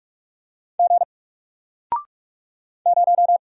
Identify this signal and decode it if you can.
Morse code, heard as a keyed tone, about 47 wpm, 700 Hz. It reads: G 0
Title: Single Character (Letter:Number:Punctuation:Prosign) Alt 45wpm